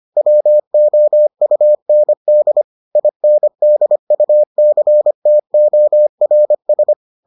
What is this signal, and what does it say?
Morse code, heard as a keyed tone, about 25 wpm, 600 Hz. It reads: WOUND INDUCTORS